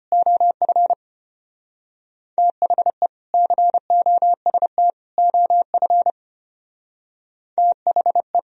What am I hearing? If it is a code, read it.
Morse code, heard as a keyed tone, about 31 wpm, 700 Hz. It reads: OF THE COST OF THE